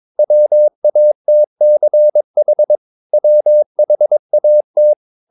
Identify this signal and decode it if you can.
Morse code, heard as a keyed tone, about 22 wpm, 600 Hz. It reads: WATCH WHAT
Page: Romanen Never af Ken Follet på engelsk.